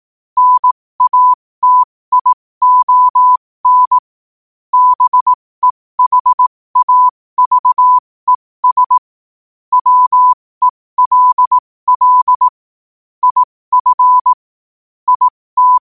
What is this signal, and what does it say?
Morse code, heard as a keyed tone, about 18 wpm, spaced as for 15 wpm, 1000 Hz. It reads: NATION BEHAVES WELL IF IT
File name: qotdfiles / 15wpm